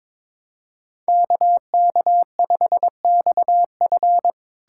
Morse code, heard as a keyed tone, about 22 wpm, 700 Hz. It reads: KK5XF
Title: USA Callsign DE Callsign 22wpm